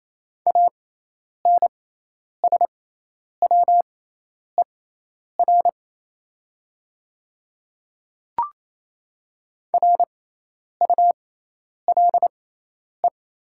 Morse code, heard as a keyed tone, about 28 wpm, spaced as for 10 wpm, 700 Hz. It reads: ANSWER RULE
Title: 6x Character Spacing - Top 500 Words - 28wpm